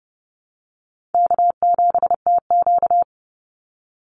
Morse code, heard as a keyed tone, about 30 wpm, 700 Hz. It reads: K7TQ
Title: NA SOTA Chaser Callsigns 30wpm